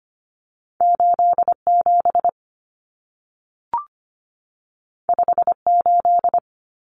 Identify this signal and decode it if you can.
Morse code, heard as a keyed tone, about 25 wpm, 700 Hz. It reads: 87 58
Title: Numbers - 2-Digit 25wpm